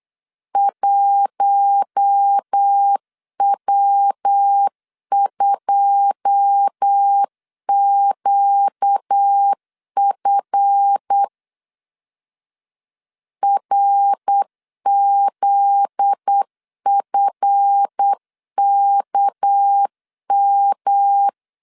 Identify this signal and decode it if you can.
Morse code, heard as a keyed tone, about 8 wpm, 800 Hz. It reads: 1W2QF RZFKM
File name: LEZIONI / 9 2 8wpm 3437